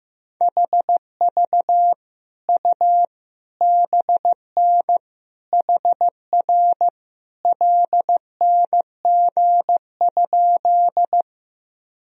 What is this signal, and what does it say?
Morse code, heard as a keyed tone, about 15 wpm, 700 Hz. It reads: HV U BN HR LNG?
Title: Sentences from Top 200 Words and Abbreviations 15wpm